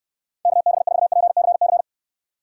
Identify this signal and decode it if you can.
Morse code, heard as a keyed tone, about 68 wpm, 700 Hz. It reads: KB4PCC